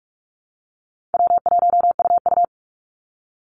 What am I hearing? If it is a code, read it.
Morse code, heard as a keyed tone, about 46 wpm, 700 Hz. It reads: W1UU
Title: Easy Call Signs - 45wpm